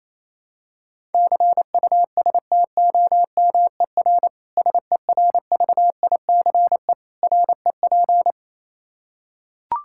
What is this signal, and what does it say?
Morse code, heard as a keyed tone, about 28 wpm, 700 Hz. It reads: CUSTOMER SERVICE REP
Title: Practice QSO - Exchange 3 - More Info - 28wpm